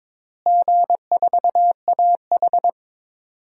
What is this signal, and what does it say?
Morse code, heard as a keyed tone, about 22 wpm, 700 Hz. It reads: G4AH
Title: DXCC-100 Call Signs - 4 Characters - Repeated 3x - 22wpm